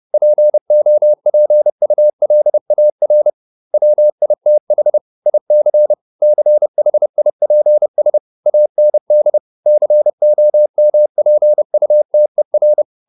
Morse code, heard as a keyed tone, about 30 wpm, 600 Hz. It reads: POPULAR WITH IC CHIPS AND COMPUTER